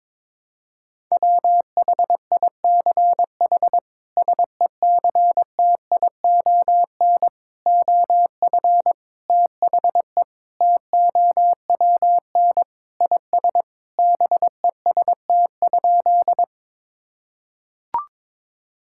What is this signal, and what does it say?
Morse code, heard as a keyed tone, about 22 wpm, 700 Hz. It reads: WHICH SECTION OF THE TOWN IS BEST?
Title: Sentences from Top 1000 Words - Review Part 5 - 22wpm